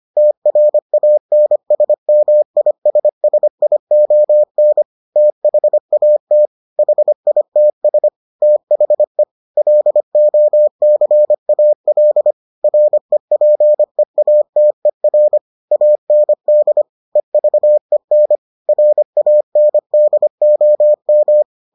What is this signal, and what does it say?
Morse code, heard as a keyed tone, about 25 wpm, 600 Hz. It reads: TRANSMISSION THAT HITS THE LOCAL REPEATER AND EVEN RANDOM